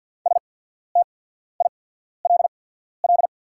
Morse code, heard as a keyed tone, about 48 wpm, spaced as for 14 wpm, 690 Hz. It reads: STILL
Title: 8x Character Spacing - Top 300 Words - 50wpm